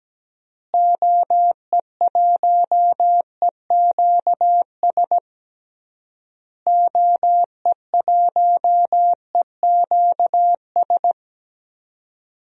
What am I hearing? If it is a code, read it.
Morse code, heard as a keyed tone, about 17 wpm, 700 Hz. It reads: OE1EQS OE1EQS
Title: DXCC-100 Call Signs - 6 Characters - Repeated 3x - 17wpm